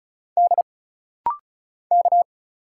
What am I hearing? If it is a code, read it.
Morse code, heard as a keyed tone, about 36 wpm, 700 Hz. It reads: D K
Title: Single Letters - Rapid-Fire 35wpm